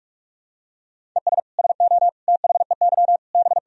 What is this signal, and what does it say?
Morse code, heard as a keyed tone, about 46 wpm, 700 Hz. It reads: ES SO THEY B
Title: Sentences from Top 400 Words and Abbreviations 45wpm